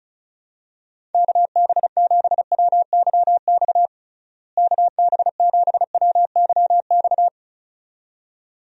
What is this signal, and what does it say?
Morse code, heard as a keyed tone, about 36 wpm, 700 Hz. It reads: KB7WYX KB7WYX
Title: USA Call Signs - 6 Characters - Repeated 2x - 35wpm